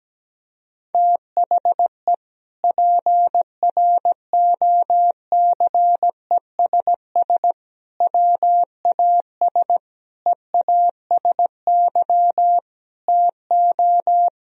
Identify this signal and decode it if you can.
Morse code, heard as a keyed tone, about 17 wpm, 700 Hz. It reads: THE PROCESS WAS EASY TO
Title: Sentences from Top 1000 Words - Review Part 3 - 17wpm